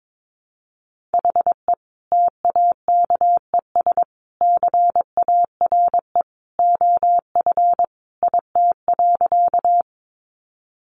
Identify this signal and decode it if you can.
Morse code, heard as a keyed tone, about 22 wpm, 700 Hz. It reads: HE TAKES CARE OF IT.
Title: Additional Easy Phrases 22wpm